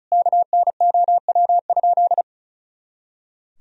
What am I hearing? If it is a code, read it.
Morse code, heard as a keyed tone, about 34 wpm, 700 Hz. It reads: KNOW?
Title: Sentences from Top 200 Words and Abbreviations 35wpm